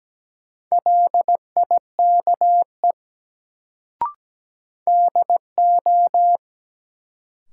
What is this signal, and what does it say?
Morse code, heard as a keyed tone, about 17 wpm, 700 Hz. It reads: LIKE DO
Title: Sentences from Top 100 Words - New Word Drill - 17wpm